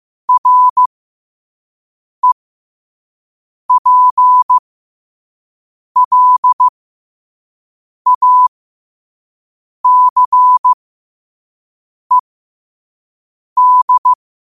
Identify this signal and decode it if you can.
Morse code, heard as a keyed tone, about 15 wpm, spaced as for 5 wpm, 1000 Hz. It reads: REPLACED
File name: qotdfiles / 5wpm